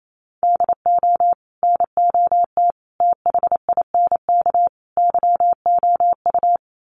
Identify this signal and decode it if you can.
Morse code, heard as a keyed tone, about 28 wpm, 700 Hz. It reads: DO NOT THINK YOU
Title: Sets of 4 Words 28wpm